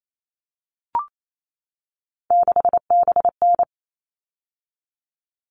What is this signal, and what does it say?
Morse code, heard as a keyed tone, about 28 wpm, 700 Hz. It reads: 6BN
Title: All Cut Numbers - 3-Digit 28wpm